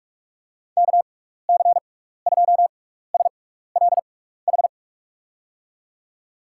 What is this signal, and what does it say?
Morse code, heard as a keyed tone, about 46 wpm, spaced as for 16 wpm, 700 Hz. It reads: KC2SLH
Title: USA Call Signs - 6 Characters - 6x Character Spacing - 45wpm